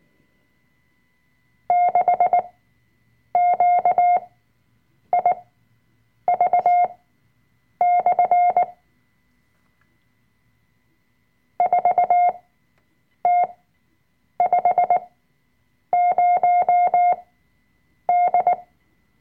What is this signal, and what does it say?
Morse code, heard as a keyed tone, about 19 wpm, spaced as for 7 wpm, 710 Hz. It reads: 6QIV/ 4T50D
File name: Workshop Dateien / CW-Online-Kurs / Lektion 005